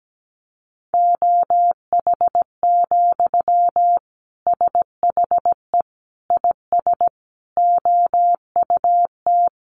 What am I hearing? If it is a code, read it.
Morse code, heard as a keyed tone, about 17 wpm, 700 Hz. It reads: OH, SHE IS OUT
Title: Sentences from Top 1100 Words 17wpm